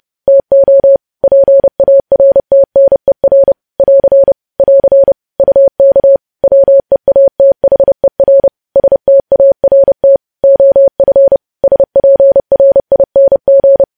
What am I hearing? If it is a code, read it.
Morse code, heard as a keyed tone, about 30 wpm, 550 Hz. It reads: TO PARTNER <AR> <AR> UK WEATHER START OF SPRING
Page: news headlines in morse code at 30 wpm